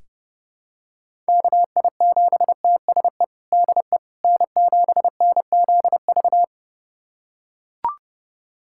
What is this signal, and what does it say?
Morse code, heard as a keyed tone, about 30 wpm, 700 Hz. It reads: KI7TSE DE N7NZV
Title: USA Callsign DE Callsign 30wpm